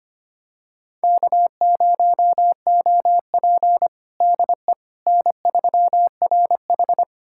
Morse code, heard as a keyed tone, about 25 wpm, 700 Hz. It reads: K0OP DE N3RH
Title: USA Callsign DE Callsign - 4 Characters - 25wpm